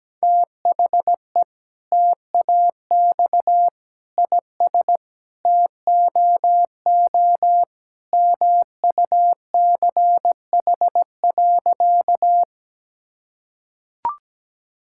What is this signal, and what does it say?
Morse code, heard as a keyed tone, about 17 wpm, 700 Hz. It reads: THE TAX IS TOO MUCH.
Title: Sentences from Top 1000 Words - Review Part 5 - 17wpm